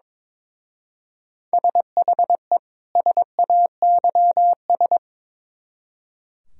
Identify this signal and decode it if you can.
Morse code, heard as a keyed tone, about 24 wpm, 700 Hz. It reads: SHE SAYS
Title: Speed-Racing - Sets of 2 Words 22wpm